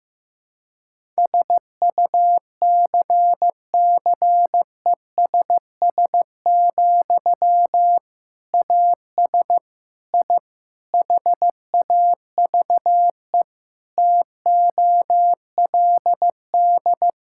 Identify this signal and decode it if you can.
Morse code, heard as a keyed tone, about 15 wpm, 700 Hz. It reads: SUCCESS, AS I HAVE TOLD
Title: Sentences from Top 1000 Words - Review Part 6 - 15wpm